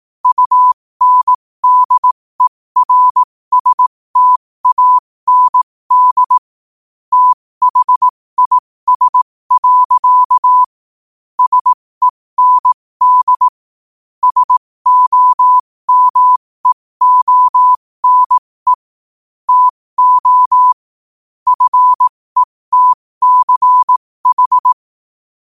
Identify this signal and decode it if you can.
Morse code, heard as a keyed tone, about 18 wpm, spaced as for 15 wpm, 1000 Hz. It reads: UNDERSTAND THIS. SEND SOMEONE TO FETCH
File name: qotdfiles / 15wpm